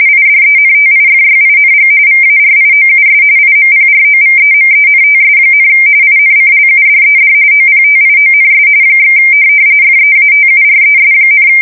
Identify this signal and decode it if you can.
RTTY, about 45.45 baud. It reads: IDENTIFYING VARIOUS IMPROVEMENTS, INCLUDED OPTIONS / FEATURES,
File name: fsk170 / teletype-corp